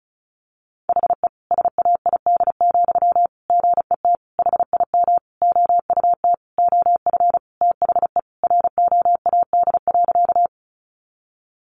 Morse code, heard as a keyed tone, about 34 wpm, 700 Hz. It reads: HE SAID, GET HIM OUT OF THE ROAD.
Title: Sentences from Top 400 Words 35wpm